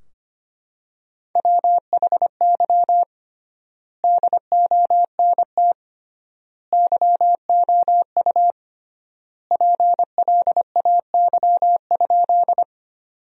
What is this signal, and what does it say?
Morse code, heard as a keyed tone, about 25 wpm, 700 Hz. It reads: WHY DONT YOU PLAY?
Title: 3x Word-Spacing - Sentences from Top 300 Words - 25wpm